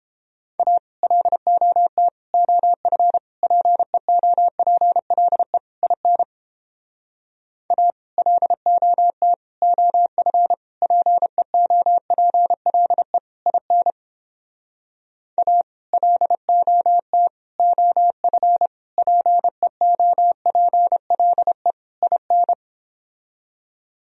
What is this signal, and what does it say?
Morse code, heard as a keyed tone, about 31 wpm, 700 Hz. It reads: A LOT OF PEOPLE IN A LOT OF PEOPLE IN A LOT OF PEOPLE IN